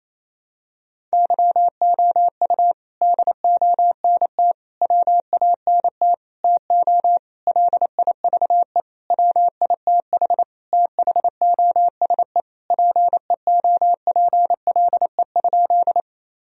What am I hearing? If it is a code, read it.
Morse code, heard as a keyed tone, about 28 wpm, 700 Hz. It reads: YOU DONT WANT TO LIVE WITH THOSE PEOPLE?
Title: Sentences from Top 1000 Words - Review Part 1 - 28wpm